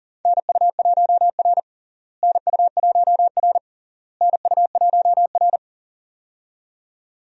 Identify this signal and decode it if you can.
Morse code, heard as a keyed tone, about 37 wpm, 700 Hz. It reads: NU1R NU1R NU1R
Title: USA Call Signs - 4 Characters - Repeated 3x - 40wpm